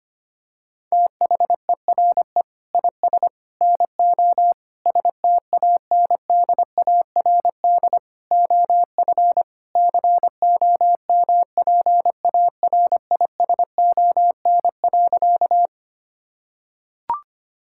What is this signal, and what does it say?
Morse code, heard as a keyed tone, about 25 wpm, 700 Hz. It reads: THERE IS NO STANDARD OF COMPARISON.
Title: Sentences from Top 1600 Words 25wpm